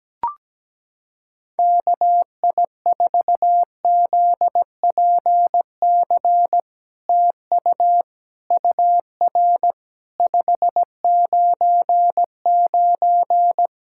Morse code, heard as a keyed tone, about 17 wpm, 700 Hz. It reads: KI4ZPC TU UR 599
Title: SOTA QSO Practice 17wpm